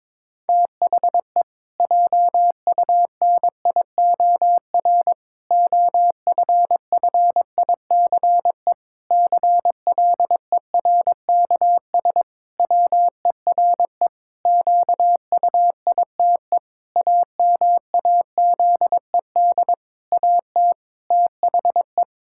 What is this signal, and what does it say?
Morse code, heard as a keyed tone, about 21 wpm, 700 Hz. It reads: THE JUNIOR OFFICE CLERKS WERE QUITE AMAZED AT THE
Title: Pangram Challenge v2 - 22wpm